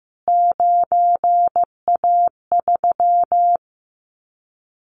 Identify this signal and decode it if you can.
Morse code, heard as a keyed tone, about 15 wpm, 700 Hz. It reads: 9A3
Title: Callsign Prefixes for Top 25 DXCC - Repeated 3x - 15wpm